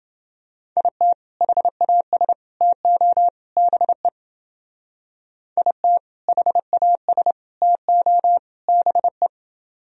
Morse code, heard as a keyed tone, about 29 wpm, 700 Hz. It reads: IT HAS TO BE IT HAS TO BE